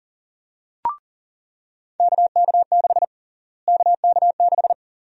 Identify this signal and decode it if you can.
Morse code, heard as a keyed tone, about 40 wpm, 700 Hz. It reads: KK6 KK6
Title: USA Call Sign Prefixes - Repeated 2x - 40wpm